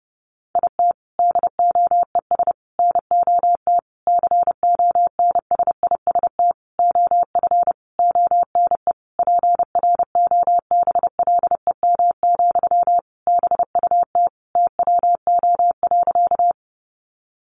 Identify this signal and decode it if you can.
Morse code, heard as a keyed tone, about 30 wpm, 700 Hz. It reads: IT DOES NOT CONSIST OF ONE PROBLEM, BUT TWO.
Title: Sentences from Top 1400 Words 30wpm